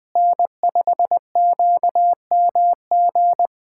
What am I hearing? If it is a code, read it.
Morse code, heard as a keyed tone, about 21 wpm, 700 Hz. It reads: N5QMG
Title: USA Callsign DE Callsign - 5 Characters - 20wpm